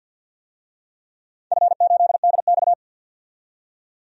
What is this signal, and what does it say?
Morse code, heard as a keyed tone, about 51 wpm, 700 Hz. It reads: F8DX
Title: DXCC-100 Call Signs - 4 Characters - 50wpm